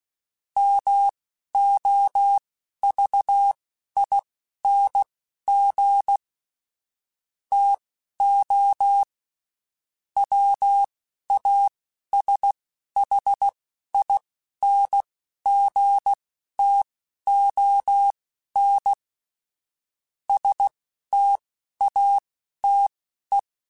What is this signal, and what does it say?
Morse code, heard as a keyed tone, about 16 wpm, spaced as for 11 wpm, 800 Hz. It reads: MOVING TO WASHINGTON STATE